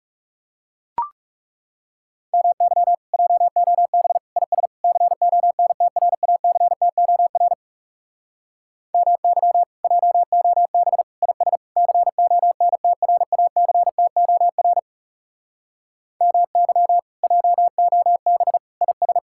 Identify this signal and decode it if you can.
Morse code, heard as a keyed tone, about 37 wpm, 700 Hz. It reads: MY JOB IS CONTRACTOR MY JOB IS CONTRACTOR MY JOB IS
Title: Speed-Racing Common QSO Phrases - 30wpm